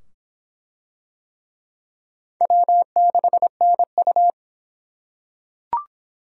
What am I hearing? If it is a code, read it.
Morse code, heard as a keyed tone, about 25 wpm, 700 Hz. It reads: W6NU